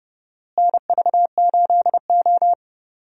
Morse code, heard as a keyed tone, about 30 wpm, 700 Hz. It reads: NV8O
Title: USA Call Signs - 4 Characters - Repeated 3x - 30wpm